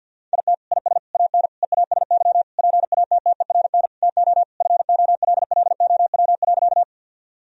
Morse code, heard as a keyed tone, about 51 wpm, 700 Hz. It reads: IT IS AN EASY PATTERN TO FOLLOW.
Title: Sentences from Top 1000 Words - Review Part 2 - 50wpm